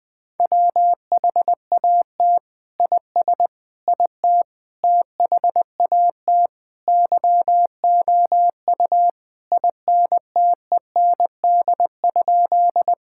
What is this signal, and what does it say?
Morse code, heard as a keyed tone, about 20 wpm, 700 Hz. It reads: WHAT IS IT THAT YOU INTEND?
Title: Sentences from Top 1600 Words 20wpm